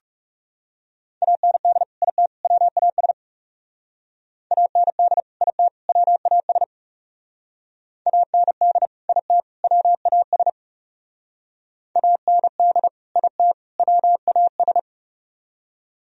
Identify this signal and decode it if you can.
Morse code, heard as a keyed tone, about 37 wpm, 700 Hz. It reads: AND IT WAS AND IT WAS AND IT WAS AND IT WAS
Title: Speed-Racing - Sets of 3 Words 30wpm